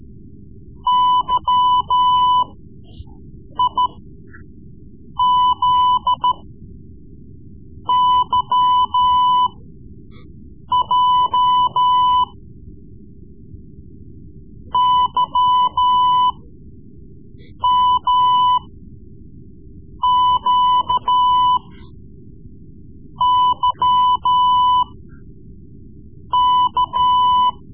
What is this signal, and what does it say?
Morse code, heard as a keyed tone, about 13 wpm, spaced as for 5 wpm, 1000 Hz. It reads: YIZYJ YMQYK
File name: Morse Lessons / Morse Lessons / Tape-3B